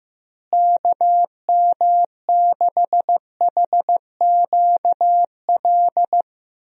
Morse code, heard as a keyed tone, about 15 wpm, 700 Hz. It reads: KM6HQL
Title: USA Call Signs - 6 Characters - Repeated 3x - 15wpm